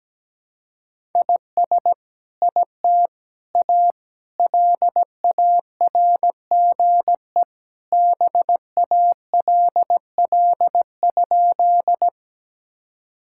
Morse code, heard as a keyed tone, about 17 wpm, 700 Hz. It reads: IS IT A LARGE BALL?
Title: Sentences from Top 1000 Words - Review Part 6 - 17wpm